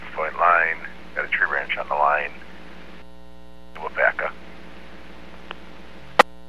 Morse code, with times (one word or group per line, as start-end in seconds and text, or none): none